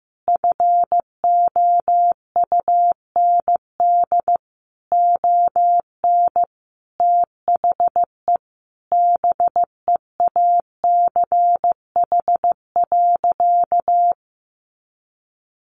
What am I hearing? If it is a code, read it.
Morse code, heard as a keyed tone, about 15 wpm, 700 Hz. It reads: FOUND ON THE BEACH.
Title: Sentences from Top 1500 Words 15wpm